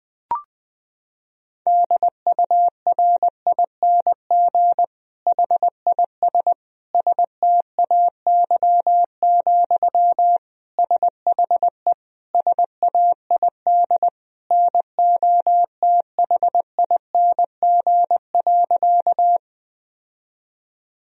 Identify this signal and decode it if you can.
Morse code, heard as a keyed tone, about 20 wpm, 700 Hz. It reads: DURING HIS STAY, SHE SAID NOTHING.
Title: Sentences from Top 1000 Words - Review Part 2 - 20wpm